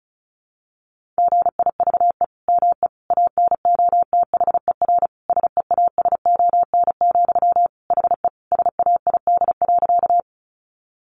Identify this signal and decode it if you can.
Morse code, heard as a keyed tone, about 34 wpm, 700 Hz. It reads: GIVE ME ANOTHER SEASON, HE SAID.
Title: Sentences from Top 1400 Words 35wpm